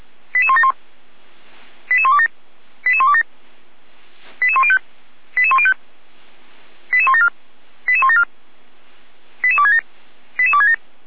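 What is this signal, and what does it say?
Five-tone selective call (ZVEI), heard as quick runs of five tones, tones of about 70 ms. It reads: 88171 88127 88127 88105 88105 88174 88174 88267 88267
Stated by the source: ZVEI five-tone call received by SDR, from audio.f85360.feuerwehr-orlamuende.org, a fire brigade (Feuerwehr Orlamünde) server.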